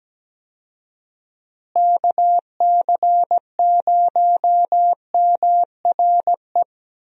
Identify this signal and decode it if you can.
Morse code, heard as a keyed tone, about 17 wpm, 700 Hz. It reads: KC0MRE
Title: USA Call Signs - 6 Characters - Repeated 2x - 17wpm